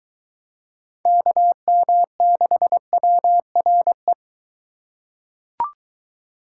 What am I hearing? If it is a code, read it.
Morse code, heard as a keyed tone, about 21 wpm, 700 Hz. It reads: KM6WRE